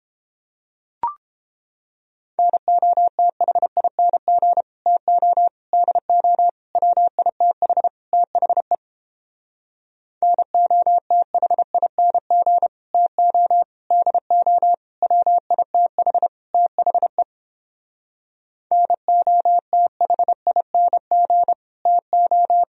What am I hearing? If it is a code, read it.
Morse code, heard as a keyed tone, about 29 wpm, 700 Hz. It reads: NOTHING TO DO WITH THE NOTHING TO DO WITH THE NOTHING TO